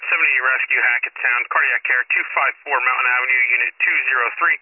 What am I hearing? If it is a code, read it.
Emergency Alert System (SAME) tones, heard as harsh buzzing bursts, with speech. It is heard from 0.6 to 3.4 s.